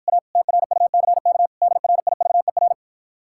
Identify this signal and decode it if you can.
Morse code, heard as a keyed tone, about 54 wpm, 700 Hz. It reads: A TRUCK DRIVER